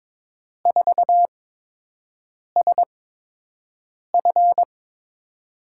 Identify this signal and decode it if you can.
Morse code, heard as a keyed tone, about 22 wpm, spaced as for 6 wpm, 700 Hz. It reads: 4SF
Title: USA Call Signs - 5 Characters - 8x Character Spacing - 22wpm